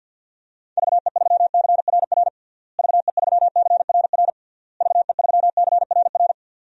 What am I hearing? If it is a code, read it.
Morse code, heard as a keyed tone, about 51 wpm, 700 Hz. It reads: VE3CRR VE3CRR VE3CRR
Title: DXCC-100 Call Signs - 6 Characters - Repeated 3x - 50wpm